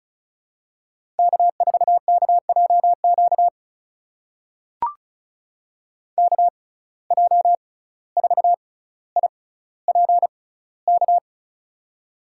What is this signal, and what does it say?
Morse code, heard as a keyed tone, about 36 wpm, spaced as for 17 wpm, 700 Hz. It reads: K4KJQ KJ4IPK
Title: USA Call Signs - 6x Character Spacing - 35wpm